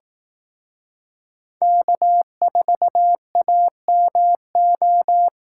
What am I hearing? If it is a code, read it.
Morse code, heard as a keyed tone, about 18 wpm, 700 Hz. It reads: K4AMO
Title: Speed Racing - USA 1x3 Call Signs 15wpm